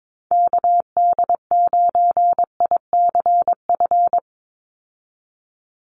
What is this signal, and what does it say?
Morse code, heard as a keyed tone, about 22 wpm, 700 Hz. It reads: KD9ICF
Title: USA Call Signs 22wpm